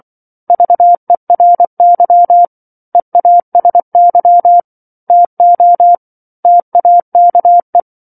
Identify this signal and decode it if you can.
Morse code, heard as a keyed tone, about 24 wpm, 700 Hz. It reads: VERY EASY TO TAKE
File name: 07-Chapter-07 24 WPM